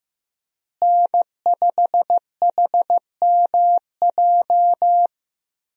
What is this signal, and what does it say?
Morse code, heard as a keyed tone, about 15 wpm, 700 Hz. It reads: N5HMJ
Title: USA Call Signs - 5 Characters - Repeated 3x - 15wpm